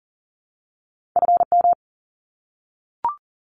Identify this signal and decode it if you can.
Morse code, heard as a keyed tone, about 40 wpm, 700 Hz. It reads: FM
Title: Abbreviations 40wpm